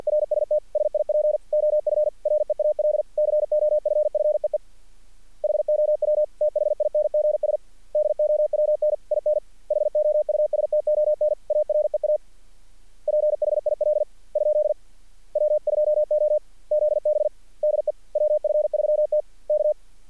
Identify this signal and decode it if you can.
Morse code, heard as a keyed tone, about 50 wpm, 600 Hz. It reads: RST SIP OF REAL COFFEE HOW THINGS DOWN IN HOUSTON AREA PHIL ? W2OZB DE WF3T K
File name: archived / CRSnet